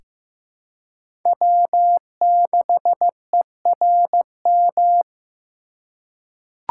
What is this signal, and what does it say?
Morse code, heard as a keyed tone, about 15 wpm, 700 Hz. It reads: W6ERM
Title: USA Call Signs - 5 Characters - 4x Character Spacing - 15wpm